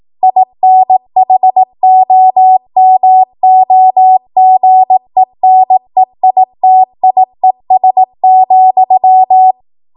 Morse code, heard as a keyed tone, about 18 wpm, 750 Hz. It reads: INHOMOGENEITIES,